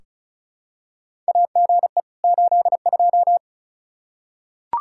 Morse code, heard as a keyed tone, about 36 wpm, 700 Hz. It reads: AGE 82
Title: Practice QSO - Exchange 3 - More Info - 35wpm